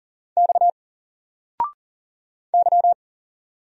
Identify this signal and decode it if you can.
Morse code, heard as a keyed tone, about 40 wpm, 700 Hz. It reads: X Y
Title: Single Letter-Number Alt 40wpm